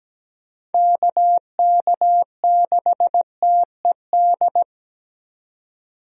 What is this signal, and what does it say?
Morse code, heard as a keyed tone, about 17 wpm, 700 Hz. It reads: KK6TED
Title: USA Call Signs - 6 Characters - 17wpm